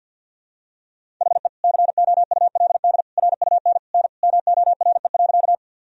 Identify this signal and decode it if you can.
Morse code, heard as a keyed tone, about 47 wpm, 700 Hz. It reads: HE COULD RUN N MORE.
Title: Sentences from Top 300 Words and Abbreviations 50wpm